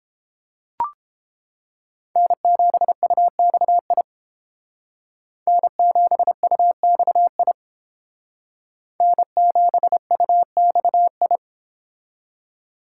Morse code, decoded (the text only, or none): N7UXI N7UXI N7UXI